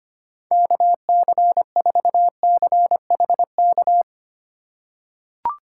Morse code, heard as a keyed tone, about 25 wpm, 700 Hz. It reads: KC4CHK
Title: USA Call Signs - 3x Character Spacing - 25wpm